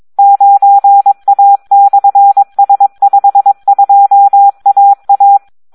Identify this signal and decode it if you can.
Morse code, heard as a keyed tone, about 22 wpm, 800 Hz. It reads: 9A/S52AA
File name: Audio practica CW / SD 1 22